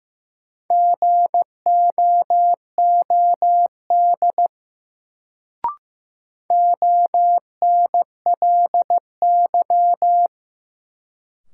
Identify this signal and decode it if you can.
Morse code, heard as a keyed tone, about 15 wpm, 700 Hz. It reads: GOOD ONLY
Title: Sets of 2 Words - New Word Drill - 15wpm